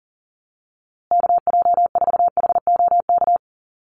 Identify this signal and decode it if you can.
Morse code, heard as a keyed tone, about 40 wpm, 700 Hz. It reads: KJ4HOK
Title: USA Call Signs Repeated 2x 40wpm